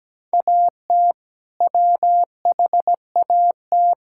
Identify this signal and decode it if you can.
Morse code, heard as a keyed tone, about 17 wpm, 700 Hz. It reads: AT WHAT
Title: Sentences from Top 1600 Words 17wpm